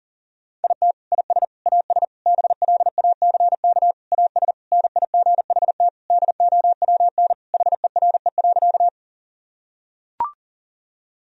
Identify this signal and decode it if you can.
Morse code, heard as a keyed tone, about 37 wpm, 700 Hz. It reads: IT IS AS BLACK AS NIGHT DOWN HERE.
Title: Sentences from Top 1000 Words - Review Part 2 - 40wpm